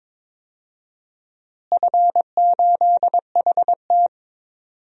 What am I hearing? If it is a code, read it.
Morse code, heard as a keyed tone, about 22 wpm, 700 Hz. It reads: F8HT
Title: DXCC-100 Call Signs - 4 Characters - Repeated 3x - 22wpm